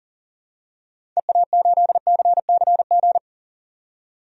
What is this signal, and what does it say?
Morse code, heard as a keyed tone, about 40 wpm, 700 Hz. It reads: EA8CCG
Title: DXCC-100 Call Signs - 6 Characters - 40wpm